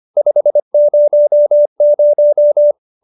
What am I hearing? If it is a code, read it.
Morse code, heard as a keyed tone, about 25 wpm, 590 Hz. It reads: 500
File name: QSO315 25